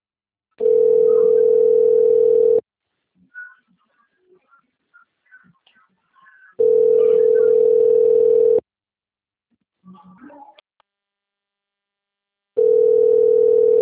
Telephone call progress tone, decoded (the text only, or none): ringback tone